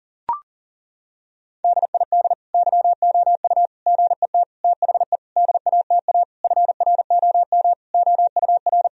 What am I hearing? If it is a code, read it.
Morse code, heard as a keyed tone, about 39 wpm, 700 Hz. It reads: DID YOU GET THE DATA FROM OUR
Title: Sentences from Top 1000 Words - Review Part 3 - 40wpm